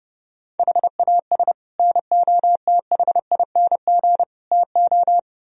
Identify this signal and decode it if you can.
Morse code, heard as a keyed tone, about 36 wpm, 700 Hz. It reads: HAS NOTHING TO
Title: Speed-Racing - Sets of 5 Words 30wpm